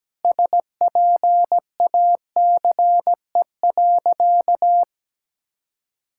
Morse code, heard as a keyed tone, about 17 wpm, 700 Hz. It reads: SPACE.
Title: Sentences from Top 1000 Words - Review Part 5 - 17wpm